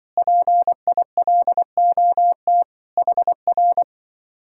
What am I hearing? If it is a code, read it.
Morse code, heard as a keyed tone, about 24 wpm, 700 Hz. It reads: PILOT HR
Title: Speed-Racing Common QSO Phrases Encore - 20wpm